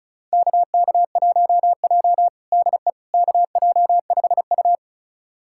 Morse code, heard as a keyed tone, about 36 wpm, 700 Hz. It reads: KK1J DE KJ5U
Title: USA Callsign DE Callsign - 4 Characters - 35wpm